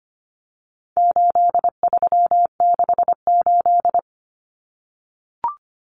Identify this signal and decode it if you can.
Morse code, heard as a keyed tone, about 25 wpm, 700 Hz. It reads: 8368
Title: Numbers - 4-Digit 25wpm